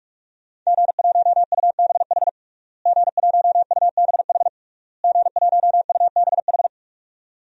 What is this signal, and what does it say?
Morse code, heard as a keyed tone, about 46 wpm, 700 Hz. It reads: G1UBH G1UBH G1UBH
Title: DXCC-100 Call Signs - 5 Characters - Repeated 3x - 45wpm